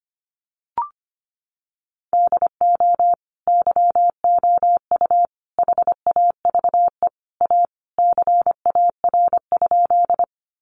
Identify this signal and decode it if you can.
Morse code, heard as a keyed tone, about 25 wpm, 700 Hz. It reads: DO YOU HAVE A CAR?
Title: Sentences from Top 300 Words 25wpm